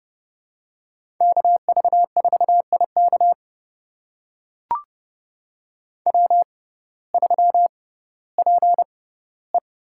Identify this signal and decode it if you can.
Morse code, heard as a keyed tone, about 30 wpm, spaced as for 16 wpm, 700 Hz. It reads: KV4IK W3PE
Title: USA Call Signs - 5 Characters - 6x Character Spacing - 30wpm